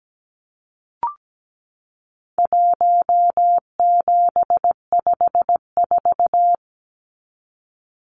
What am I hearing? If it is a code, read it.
Morse code, heard as a keyed tone, about 17 wpm, 700 Hz. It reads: 1754
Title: Numbers - Contest Style - 4-Digit 17wpm